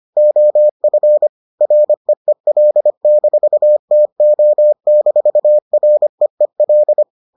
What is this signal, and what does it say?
Morse code, heard as a keyed tone, about 25 wpm, 600 Hz. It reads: OF REEL-TO-REEL